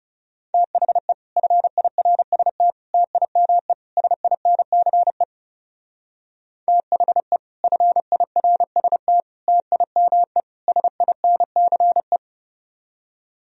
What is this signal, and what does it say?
Morse code, heard as a keyed tone, about 36 wpm, 700 Hz. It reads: THE FIRST TIME SINCE THE FIRST TIME SINCE
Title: Speed-Racing - Sets of 4 Words 30wpm